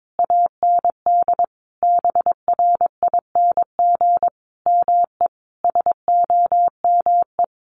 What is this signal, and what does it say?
Morse code, heard as a keyed tone, about 22 wpm, 700 Hz. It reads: AND BRING ME SOME
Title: Sentences from Top 1200 Words 22wpm